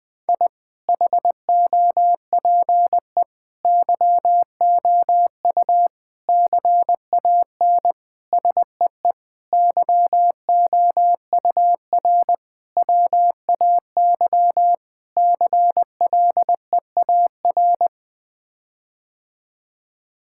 Morse code, heard as a keyed tone, about 21 wpm, 700 Hz. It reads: I HOPE YOU CAN SEE YOUR WAY CLEAR
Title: Sets of 8 Words - 20wpm